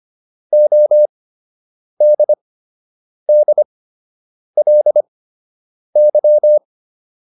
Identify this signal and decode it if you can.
Morse code, heard as a keyed tone, about 25 wpm, spaced as for 8 wpm, 600 Hz. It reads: ODDLY